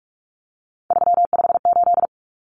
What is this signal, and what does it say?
Morse code, heard as a keyed tone, about 44 wpm, 700 Hz. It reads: 358
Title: Numbers - 3-Digit 45wpm